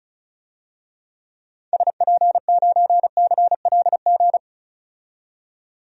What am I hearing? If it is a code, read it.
Morse code, heard as a keyed tone, about 36 wpm, 700 Hz. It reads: SP9CLG
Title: DXCC-100 Call Signs - 6 Characters - 35wpm